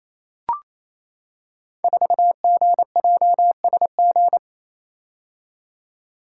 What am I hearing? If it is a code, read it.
Morse code, heard as a keyed tone, about 28 wpm, 700 Hz. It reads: 4GJSG
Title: Code Groups - 5 Letter:Numbers - 28wpm